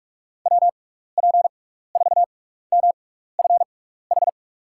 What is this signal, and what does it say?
Morse code, heard as a keyed tone, about 42 wpm, spaced as for 15 wpm, 700 Hz. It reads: WP4MFH
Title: USA Call Signs - 6 Characters - 6x Character Spacing - 45wpm